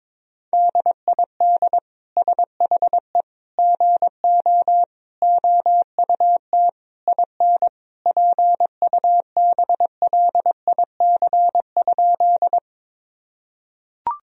Morse code, heard as a keyed tone, about 22 wpm, 700 Hz. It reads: DID SHE GO OUT IN PUBLIC?
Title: Sentences from Top 1000 Words - Review Part 3 - 22wpm